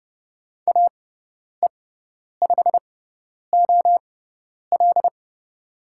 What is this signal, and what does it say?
Morse code, heard as a keyed tone, about 30 wpm, spaced as for 10 wpm, 700 Hz. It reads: AE5OL
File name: Lesson 16g - Letter c - Call Signs 30wpm